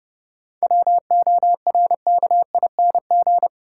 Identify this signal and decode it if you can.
Morse code, heard as a keyed tone, about 30 wpm, 700 Hz. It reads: WORKING
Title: Seven Letter Words - 30wpm